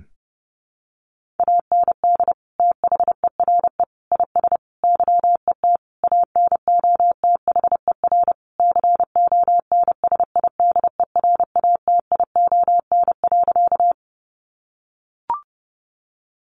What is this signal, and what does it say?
Morse code, heard as a keyed tone, about 30 wpm, 700 Hz. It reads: AND THERE IS YET ANOTHER CONSIDERATION.
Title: Sentences from Top 1500 Words 30wpm